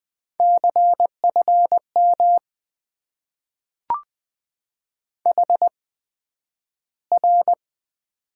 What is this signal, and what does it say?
Morse code, heard as a keyed tone, about 20 wpm, spaced as for 11 wpm, 700 Hz. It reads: CFM HR
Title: Top 90 QSO Elements - 8x Character Spacing - 20wpm